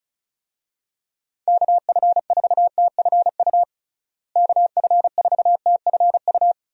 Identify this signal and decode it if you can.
Morse code, heard as a keyed tone, about 36 wpm, 700 Hz. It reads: KF4TFU KF4TFU
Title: USA Call Signs - 6 Characters - Repeated 3x - 35wpm